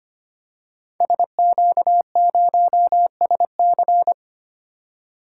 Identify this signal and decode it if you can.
Morse code, heard as a keyed tone, about 25 wpm, 700 Hz. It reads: SQ0SC
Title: Code Groups - 5 Letter:Numbers - 25wpm